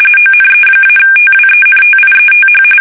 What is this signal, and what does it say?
RTTY, about 45.45 baud. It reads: ONE-TWENTIETH
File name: fsk850 / battleshipnj